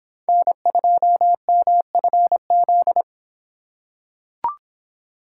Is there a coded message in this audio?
Yes: Morse code from 0.3 to 3.0 s, heard as a keyed tone, about 26 wpm, 700 Hz. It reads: N2MFZ